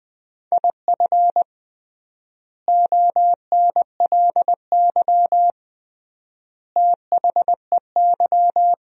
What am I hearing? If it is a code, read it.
Morse code, heard as a keyed tone, about 20 wpm, 700 Hz. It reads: IF ONLY THEY
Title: 3x Word-Spacing - Sentences from Top 500 Words - 20wpm